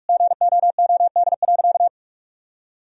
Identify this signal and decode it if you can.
Morse code, heard as a keyed tone, about 46 wpm, 700 Hz. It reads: GOOD.
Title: Sentences from Top 1000 Words - Review Part 3 - 45wpm